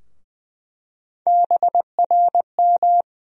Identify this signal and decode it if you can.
Morse code, heard as a keyed tone, about 20 wpm, 700 Hz. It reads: BRM
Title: USA Call Sign Postfixes - 3 Letters 20wpm